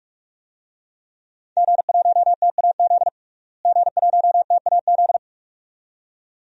Morse code, heard as a keyed tone, about 46 wpm, 700 Hz. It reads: G1TAZ G1TAZ
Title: DXCC-100 Call Signs - 5 Characters - Repeated 2x - 45wpm